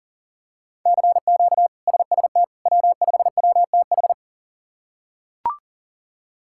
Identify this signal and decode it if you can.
Morse code, heard as a keyed tone, about 40 wpm, 700 Hz. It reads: CQ SST W5WTH
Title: K1USN SST QSO Practice - 40wpm